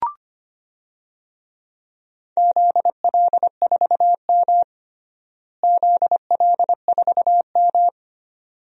Morse code, heard as a keyed tone, about 25 wpm, 700 Hz. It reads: ZL4M ZL4M
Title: DXCC-100 Call Signs - 4 Characters - Repeated 2x - 25wpm